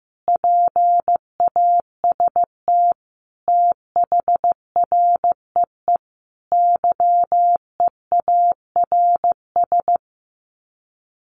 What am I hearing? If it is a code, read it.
Morse code, heard as a keyed tone, about 15 wpm, 700 Hz. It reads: PAST THREE YEARS
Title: Sets of 5 Words Encore2 15wpm